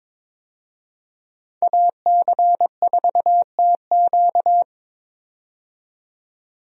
Morse code, heard as a keyed tone, about 21 wpm, 700 Hz. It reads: AC4TQ
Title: USA Call Signs - 5 Characters - 22wpm